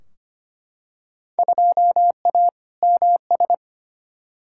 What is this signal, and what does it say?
Morse code, heard as a keyed tone, about 25 wpm, 700 Hz. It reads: 2A MS
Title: Field Day Exchanges - 25wpm